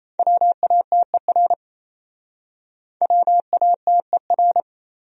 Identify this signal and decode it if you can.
Morse code, heard as a keyed tone, about 36 wpm, 700 Hz. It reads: WATER WATER